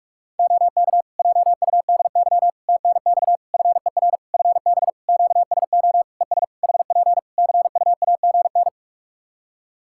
Morse code, heard as a keyed tone, about 46 wpm, 700 Hz. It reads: OK JUDY TNX FER FB QSO ES HP CUAGN
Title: Practice QSO - Exchange 4 - Ending - 45wpm